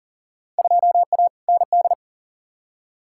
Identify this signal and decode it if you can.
Morse code, heard as a keyed tone, about 40 wpm, 700 Hz. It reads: 2A ND
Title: Field Day Exchanges Encore - 40wpm